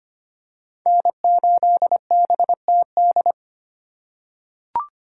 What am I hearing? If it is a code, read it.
Morse code, heard as a keyed tone, about 25 wpm, 700 Hz. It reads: N8BTD
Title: USA Call Signs - 5 Characters - 4x Character Spacing - 25wpm